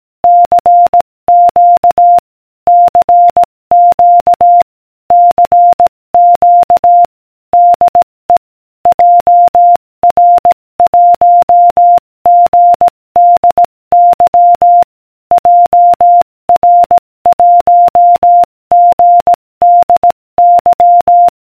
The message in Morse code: CQ CQ CQ DE JR1GDY JR1GDY